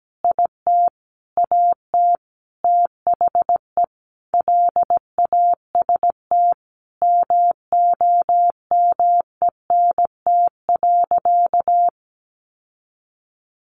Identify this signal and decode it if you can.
Morse code, heard as a keyed tone, about 17 wpm, 700 Hz. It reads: IT AT THE LAST MOMENT.
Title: Sentences from Top 1400 Words 17wpm